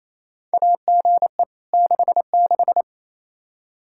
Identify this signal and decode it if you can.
Morse code, heard as a keyed tone, about 28 wpm, 700 Hz. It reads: AGE 66
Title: K1USN SST QSO Practice 28wpm